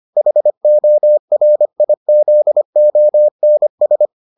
Morse code, heard as a keyed tone, about 25 wpm, 600 Hz. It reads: HORIZONS